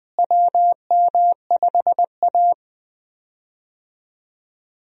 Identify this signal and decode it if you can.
Morse code, heard as a keyed tone, about 20 wpm, 700 Hz. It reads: WM5A
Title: USA Call Signs - 4 Characters - Repeated 3x - 20wpm